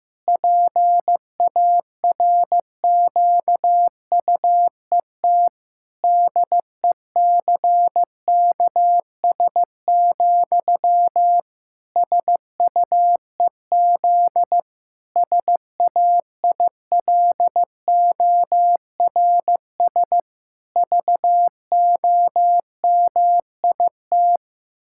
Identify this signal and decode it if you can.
Morse code, heard as a keyed tone, about 16 wpm, 700 Hz. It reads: PARQUET DECKS, SUEZ SAILORS VOMIT
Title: Pangram Challenge v2 - 15wpm